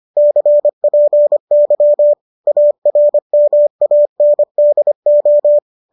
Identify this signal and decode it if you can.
Morse code, heard as a keyed tone, about 25 wpm, 590 Hz. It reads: CPY ARMANDO